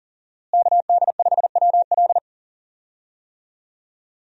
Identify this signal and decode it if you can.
Morse code, heard as a keyed tone, about 40 wpm, 700 Hz. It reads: KD5WL
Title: USA Call Signs - 5 Characters - Repeated 2x - 40wpm